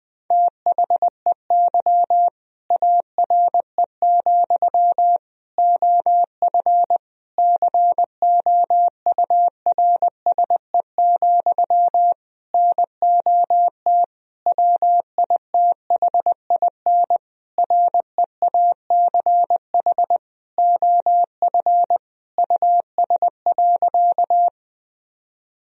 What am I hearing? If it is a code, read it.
Morse code, heard as a keyed tone, about 20 wpm, 700 Hz. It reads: THEY ARE, OF COURSE, NOT WITHIN REACH OF US.
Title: Sentences from Top 1000 Words - Review Part 4 - 20wpm